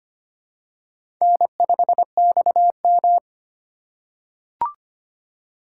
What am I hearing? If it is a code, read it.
Morse code, heard as a keyed tone, about 25 wpm, 700 Hz. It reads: N5XM
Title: USA Call Signs - 4 Characters - 3x Character Spacing - 25wpm